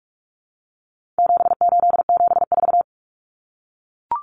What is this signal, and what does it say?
Morse code, heard as a keyed tone, about 44 wpm, 700 Hz. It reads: 7874
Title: Common Cut Numbers - 4-Digit 45wpm